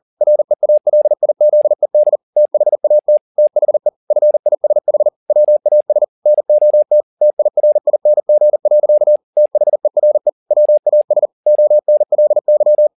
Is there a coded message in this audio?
Yes: Morse code from 0.2 to 12.9 s, heard as a keyed tone, about 40 wpm, 600 Hz. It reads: REALIZED THAT THE FISH WAS NOT TIRING. THERE WAS ONLY